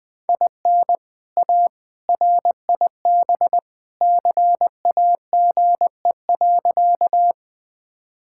Morse code, heard as a keyed tone, about 20 wpm, 700 Hz. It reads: IN A RIB CAGE.
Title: All-Age Appropriate Jokes - 20wpm